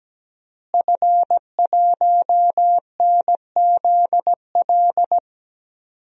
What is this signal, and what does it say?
Morse code, heard as a keyed tone, about 17 wpm, 700 Hz. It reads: F1NZL
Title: DXCC-100 Call Signs - 5 Characters - Repeated 3x - 17wpm